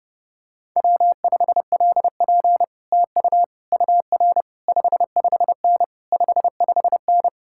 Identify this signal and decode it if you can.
Morse code, heard as a keyed tone, about 30 wpm, 700 Hz. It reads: W5LP TU UR 55N 55N